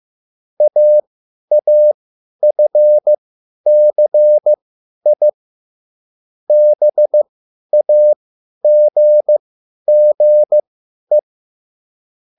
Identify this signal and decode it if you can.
Morse code, heard as a keyed tone, about 15 wpm, spaced as for 10 wpm, 600 Hz. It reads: AAFCI BAGGE